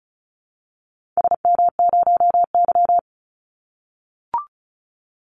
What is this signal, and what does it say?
Morse code, heard as a keyed tone, about 36 wpm, 700 Hz. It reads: SM0Y
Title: Call Signs - 35wpm